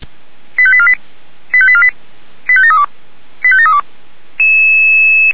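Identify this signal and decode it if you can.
Five-tone selective call (ZVEI), heard as quick runs of five tones, tones of about 70 ms. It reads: 85848 85848 85732 85732
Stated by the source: ZVEI five-tone selective call received via SDR.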